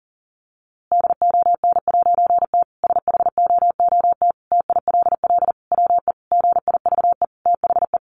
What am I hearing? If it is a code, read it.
Morse code, heard as a keyed tone, about 37 wpm, 700 Hz. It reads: DON'T SHOOT TILL WE GIVE THE
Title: A Sound Of Thunder 40wpm